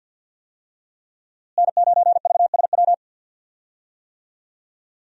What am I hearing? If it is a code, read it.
Morse code, heard as a keyed tone, about 48 wpm, 700 Hz. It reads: N9VSW
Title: USA Call Signs - 5 Characters - Repeated 2x - 50wpm